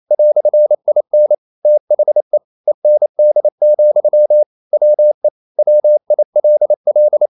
Morse code, heard as a keyed tone, about 28 wpm, 600 Hz. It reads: "IN THE END, WE WILL
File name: CW / Practice / EnglishQuotations0128